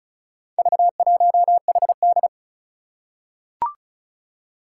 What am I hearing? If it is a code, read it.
Morse code, heard as a keyed tone, about 36 wpm, 700 Hz. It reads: V1HD
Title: Four Challenging Letter-Number Combinations 35wpm